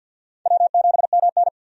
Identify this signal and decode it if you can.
Morse code, heard as a keyed tone, about 51 wpm, 700 Hz. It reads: W7MN
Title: USA Call Signs - 4 Characters - 50wpm